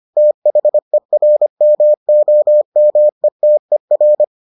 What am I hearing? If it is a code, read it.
Morse code, heard as a keyed tone, about 25 wpm, 600 Hz. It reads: THERMOMETER